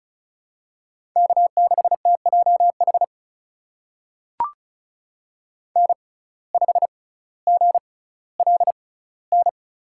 Morse code, heard as a keyed tone, about 36 wpm, spaced as for 18 wpm, 700 Hz. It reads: K6TJH N5GLN
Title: USA Call Signs - 5 Characters - 6x Character Spacing - 35wpm